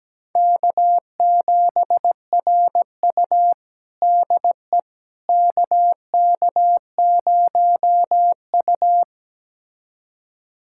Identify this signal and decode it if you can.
Morse code, heard as a keyed tone, about 17 wpm, 700 Hz. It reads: K7RU DE KK0U
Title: USA Callsign DE Callsign - 4 Characters - 17wpm